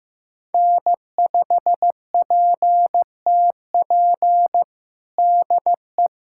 Morse code, heard as a keyed tone, about 16 wpm, 700 Hz. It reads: N5PTP DE
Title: USA Callsign DE Callsign - 5 Characters - 15wpm